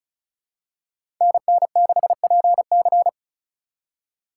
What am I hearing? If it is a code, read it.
Morse code, heard as a keyed tone, about 36 wpm, 700 Hz. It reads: NN6PC
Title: USA Call Signs - 3x Character Spacing - 35wpm